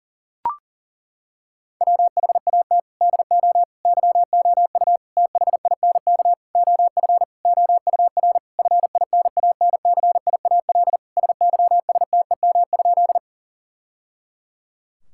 Morse code, heard as a keyed tone, about 40 wpm, 700 Hz. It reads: WHAT DO YOU THINK OF OUR FINANCIAL SYSTEM?
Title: Sentences from Top 1000 Words - Review Part 5 - 40wpm